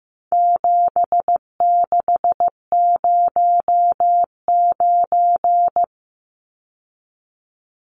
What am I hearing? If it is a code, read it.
Morse code, heard as a keyed tone, about 15 wpm, 700 Hz. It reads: 7609
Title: Numbers - 4-Digit 15wpm